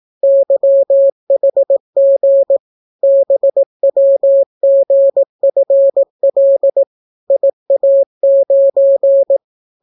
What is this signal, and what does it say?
Morse code, heard as a keyed tone, about 18 wpm, 550 Hz. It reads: YHG BWGFL IA9